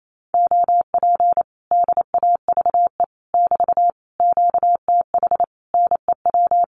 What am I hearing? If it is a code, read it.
Morse code, heard as a keyed tone, about 28 wpm, 700 Hz. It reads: OP DAVE <BT> QTH NEW
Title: Common QSO Phrases Encore 28wpm